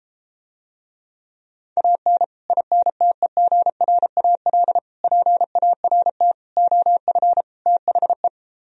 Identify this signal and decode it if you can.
Morse code, heard as a keyed tone, about 34 wpm, 700 Hz. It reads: AN INTEGRAL PART OF THE